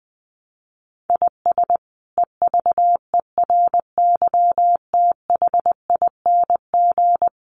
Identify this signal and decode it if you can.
Morse code, heard as a keyed tone, about 20 wpm, 700 Hz. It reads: IS EVERYTHING
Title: Sentences from Top 700 Words 20wpm